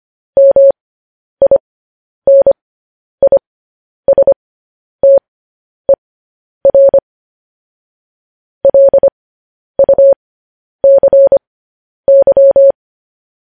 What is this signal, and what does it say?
Morse code, heard as a keyed tone, about 25 wpm, spaced as for 10 wpm, 550 Hz. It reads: MINISTER LUCY
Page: News Headlines in Morse Code at 10 WPM